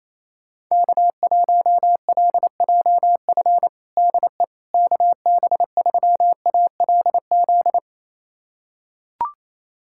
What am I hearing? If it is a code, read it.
Morse code, heard as a keyed tone, about 28 wpm, 700 Hz. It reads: K1LJF DE KB3ALZ
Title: USA Callsign DE Callsign 28wpm